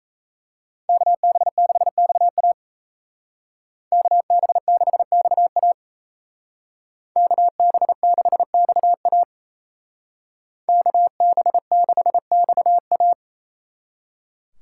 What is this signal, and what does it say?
Morse code, heard as a keyed tone, about 36 wpm, 700 Hz. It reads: KB6XA KB6XA KB6XA KB6XA